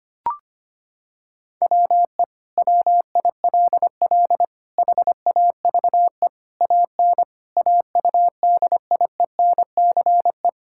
Morse code, heard as a keyed tone, about 25 wpm, 700 Hz. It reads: WE WILL HAVE AN AUDIENCE
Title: Sentences from Top 1000 Words - Review Part 5 - 25wpm